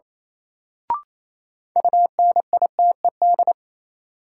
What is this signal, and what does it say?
Morse code, heard as a keyed tone, about 28 wpm, 700 Hz. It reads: UNITED
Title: Sentences from Top 600 Words - New Word Drill - 28wpm